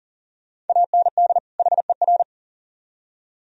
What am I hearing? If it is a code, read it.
Morse code, heard as a keyed tone, about 50 wpm, 700 Hz. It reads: AND HER
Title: Speed-Racing - Sets of 2 Words 40wpm